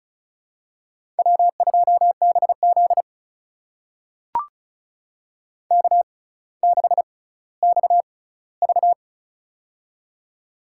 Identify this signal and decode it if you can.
Morse code, heard as a keyed tone, about 35 wpm, spaced as for 18 wpm, 700 Hz. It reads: W2BZ K6XV
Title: USA Call Signs - 4 Characters - 6x Character Spacing - 35wpm